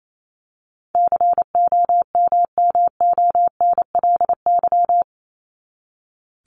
Morse code, heard as a keyed tone, about 28 wpm, 700 Hz. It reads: COMMONLY
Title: LY Suffix - Top 100 Words with Base Words 28wpm